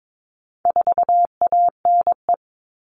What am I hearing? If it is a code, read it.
Morse code, heard as a keyed tone, about 22 wpm, 700 Hz. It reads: 4ANE
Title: All Cut Numbers - 4-Digit 22wpm